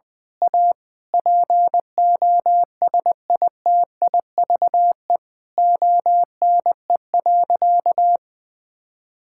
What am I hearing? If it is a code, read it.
Morse code, heard as a keyed tone, about 20 wpm, 700 Hz. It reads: A POSITIVE ONE.
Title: Sentences from Top 1000 Words - Review Part 6 - 20wpm